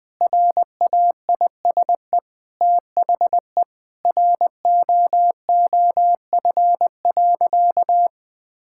Morse code, heard as a keyed tone, about 20 wpm, 700 Hz. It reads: RAISE THE ROOF.
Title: All-Age Appropriate Jokes - 20wpm